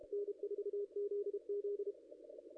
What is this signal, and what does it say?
Morse code, heard as a keyed tone, about 29 wpm, 410 Hz. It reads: N4ZZ
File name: contests / ss-cw-k5tr-op-n5rz -0600